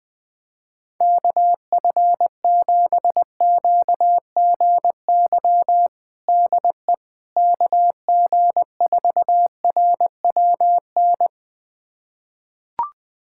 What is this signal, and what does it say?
Morse code, heard as a keyed tone, about 20 wpm, 700 Hz. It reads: KF7QGY DE KG4RWN
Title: USA Callsign DE Callsign - 6 Characters - 20wpm